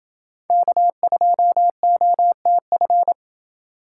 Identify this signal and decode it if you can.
Morse code, heard as a keyed tone, about 28 wpm, 700 Hz. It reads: K2OTF